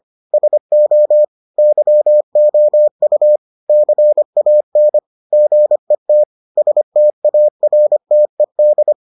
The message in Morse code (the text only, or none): SO YOU CAN GET STARTED